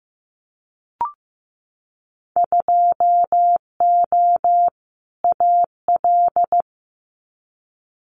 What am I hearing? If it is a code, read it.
Morse code, heard as a keyed tone, about 15 wpm, 700 Hz. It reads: 2O AL
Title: Winter Field Day Exchanges 15wpm